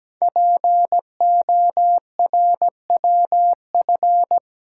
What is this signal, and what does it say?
Morse code, heard as a keyed tone, about 17 wpm, 700 Hz. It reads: PORWF
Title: Code Groups - 5 Letters - 17wpm